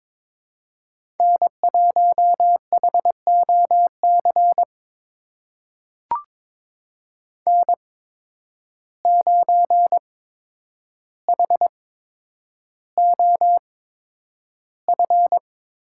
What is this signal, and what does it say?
Morse code, heard as a keyed tone, about 22 wpm, spaced as for 9 wpm, 700 Hz. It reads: N1HOC N9HOF
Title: USA Call Signs - 5 Characters - 8x Character Spacing - 22wpm